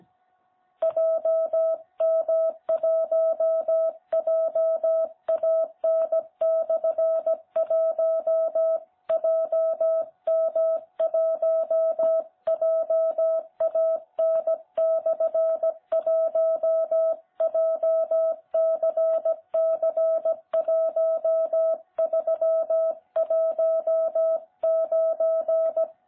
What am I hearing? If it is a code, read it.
Morse code, heard as a keyed tone, about 16 wpm, 660 Hz. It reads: JM1JAN/1 JM1JAN/1 JCC1319
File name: jh8cbh / 2021cw